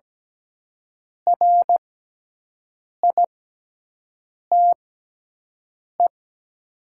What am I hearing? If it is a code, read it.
Morse code, heard as a keyed tone, about 17 wpm, spaced as for 6 wpm, 700 Hz. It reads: RITE
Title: 6x Character Spacing - Top 100 Words - 17wpm